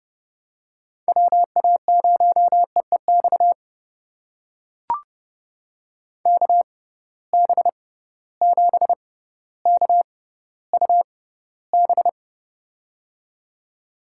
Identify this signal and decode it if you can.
Morse code, heard as a keyed tone, about 30 wpm, spaced as for 15 wpm, 700 Hz. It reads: WA0EEX KB7KUB
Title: USA Call Signs - 6 Characters - 6x Character Spacing - 30wpm